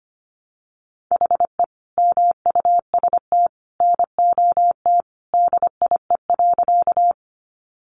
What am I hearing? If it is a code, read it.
Morse code, heard as a keyed tone, about 25 wpm, 700 Hz. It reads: HE MUST NOT DIE.
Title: Sentences from Top 1300 Words 25wpm